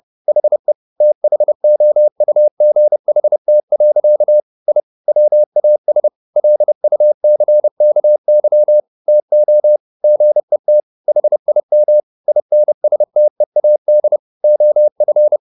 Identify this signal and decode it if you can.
Morse code, heard as a keyed tone, about 30 wpm, 600 Hz. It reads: HE THOUGHT. I WAS LUCKY TO GET HIM INSTEAD OF
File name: the old man and the sea / the old man and the sea 1 30wpm